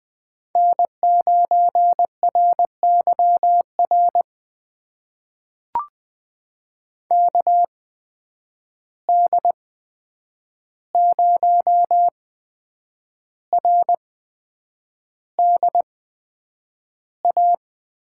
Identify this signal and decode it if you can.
Morse code, heard as a keyed tone, about 21 wpm, spaced as for 8 wpm, 700 Hz. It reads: N9RYR KD0RDA
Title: USA Call Signs - 8x Character Spacing - 20wpm